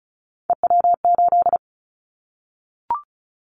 Morse code, heard as a keyed tone, about 36 wpm, 700 Hz. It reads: EW8
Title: Callsign Prefixes for Top 50 DXCC - 35wpm